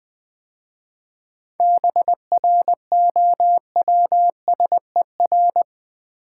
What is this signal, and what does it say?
Morse code, heard as a keyed tone, about 20 wpm, 700 Hz. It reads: BROWSER
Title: 7 Letter Words - First 3 Letters Given - 20wpm